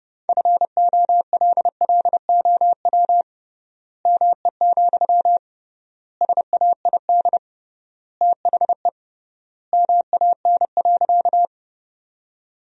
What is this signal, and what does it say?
Morse code, heard as a keyed tone, about 31 wpm, 700 Hz. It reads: FOLLOW ME, SAID THE MAN.
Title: 3x Word-Spacing - Sentences from Top 200 Words - 30wpm